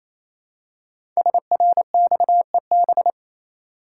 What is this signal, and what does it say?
Morse code, heard as a keyed tone, about 27 wpm, 700 Hz. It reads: SRXEB
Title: Code Groups - 5 Letters - 28wpm